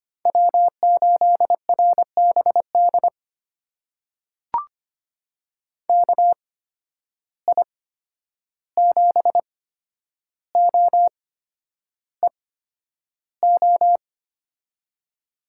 Morse code, heard as a keyed tone, about 25 wpm, spaced as for 10 wpm, 700 Hz. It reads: W8RBD KI7OEO
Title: USA Call Signs - 8x Character Spacing - 25wpm